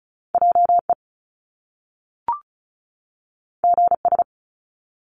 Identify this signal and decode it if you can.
Morse code, heard as a keyed tone, about 36 wpm, 700 Hz. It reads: JE GS
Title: Top Two Letter Combinations in Words 35wpm